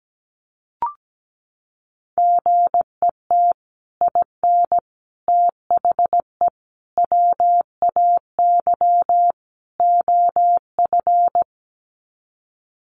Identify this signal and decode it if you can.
Morse code, heard as a keyed tone, about 17 wpm, 700 Hz. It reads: GET IN THE WAY OF
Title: Sets of 5 Words Encore 17wpm